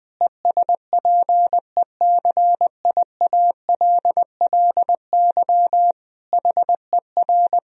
Morse code, heard as a keyed tone, about 21 wpm, 700 Hz. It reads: ESPECIALLY HER
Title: Sentences from Top 1000 Words - Review Part 4 - 20wpm